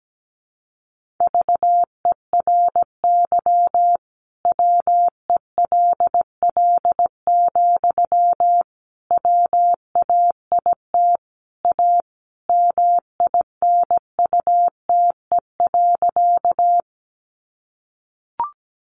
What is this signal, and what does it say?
Morse code, heard as a keyed tone, about 17 wpm, 700 Hz. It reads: VERY WELL, WAIT A MINUTE.
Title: Sentences from Top 500 Words 17wpm